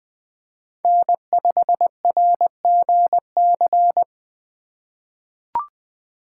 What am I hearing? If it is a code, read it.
Morse code, heard as a keyed tone, about 20 wpm, 700 Hz. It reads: N5RGC
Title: USA Call Signs - 5 Characters - 4x Character Spacing - 20wpm